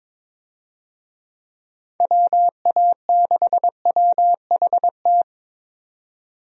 Morse code, heard as a keyed tone, about 22 wpm, 700 Hz. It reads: WA6WHT
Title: USA Call Signs - 6 Characters - 22wpm